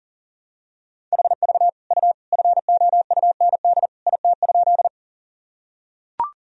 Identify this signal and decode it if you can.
Morse code, heard as a keyed tone, about 40 wpm, 700 Hz. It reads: HV U FOUND IT?
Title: Sentences from Top 300 Words and Abbreviations 40wpm